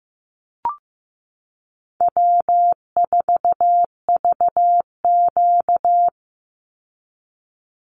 Morse code, heard as a keyed tone, about 15 wpm, 700 Hz. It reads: W4VQ
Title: Easy Call Signs - 15wpm